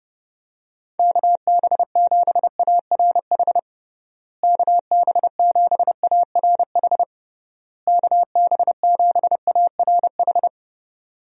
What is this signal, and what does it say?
Morse code, heard as a keyed tone, about 30 wpm, 700 Hz. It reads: KB7ARH KB7ARH KB7ARH
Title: USA Call Signs - 6 Characters - Repeated 3x - 30wpm